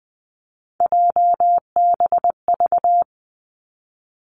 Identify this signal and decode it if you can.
Morse code, heard as a keyed tone, about 20 wpm, 700 Hz. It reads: JBV
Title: Three Challenging Letter-Number Combinations 20wpm